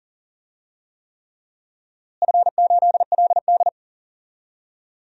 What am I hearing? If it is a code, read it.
Morse code, heard as a keyed tone, about 39 wpm, 700 Hz. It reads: F8LD
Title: DXCC-100 Call Signs - 4 Characters - Repeated 3x - 40wpm